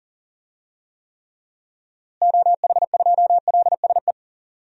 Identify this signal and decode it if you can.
Morse code, heard as a keyed tone, about 40 wpm, 700 Hz. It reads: OH2LSE
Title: DXCC-100 Call Signs - 6 Characters - Repeated 3x - 40wpm